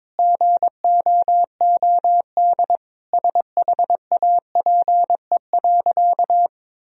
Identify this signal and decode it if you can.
Morse code, heard as a keyed tone, about 21 wpm, 700 Hz. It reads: GOOD SHAPE.
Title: Sentences from Top 1000 Words - Review Part 3 - 22wpm